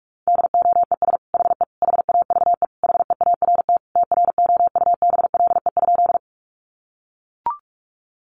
Morse code, heard as a keyed tone, about 44 wpm, 700 Hz. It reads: DOES HE HAVE HEART TROUBLE?
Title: Sentences from Top 1200 Words 45wpm